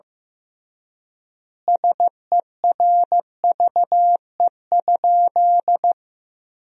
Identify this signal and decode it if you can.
Morse code, heard as a keyed tone, about 15 wpm, 700 Hz. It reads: SERVE?
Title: 3x Word-Spacing - Sentences from Top 400 Words - 15wpm